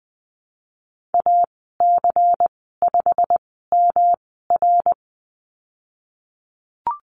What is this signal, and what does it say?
Morse code, heard as a keyed tone, about 20 wpm, spaced as for 15 wpm, 700 Hz. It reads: AC5MR
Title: 147 Lesson 24g - Letter y - Call Signs 20wpm